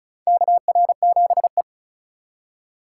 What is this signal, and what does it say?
Morse code, heard as a keyed tone, about 36 wpm, 700 Hz. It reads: KR7E
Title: USA Call Signs - 2x Character Spacing - 35wpm